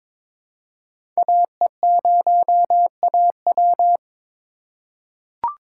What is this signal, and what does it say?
Morse code, heard as a keyed tone, about 22 wpm, 700 Hz. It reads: AE0AW
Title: USA Call Signs - 5 Characters - 8x Character Spacing - 22wpm